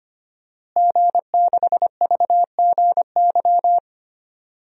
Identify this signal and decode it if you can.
Morse code, heard as a keyed tone, about 25 wpm, 700 Hz. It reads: G6VGY
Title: DXCC-100 Call Signs - 5 Characters - Repeated 2x - 25wpm